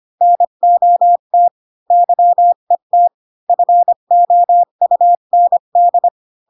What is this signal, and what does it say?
Morse code, heard as a keyed tone, about 25 wpm, 700 Hz. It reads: NOT YET FOUND
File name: Pinocchio / Pinocchio ch7 20 25 ik1hge